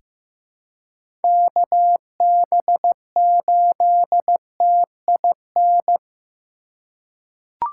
Morse code, heard as a keyed tone, about 15 wpm, 700 Hz. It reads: KB8TIN
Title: USA Call Signs - 6 Characters - 2x Character Spacing - 15wpm